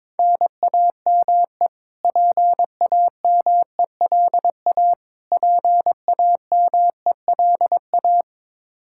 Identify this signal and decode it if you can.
Morse code, heard as a keyed tone, about 21 wpm, 700 Hz. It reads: NAME PAMELA PAMELA
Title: Practice QSO - Exchange 1 - Beginning - 22wpm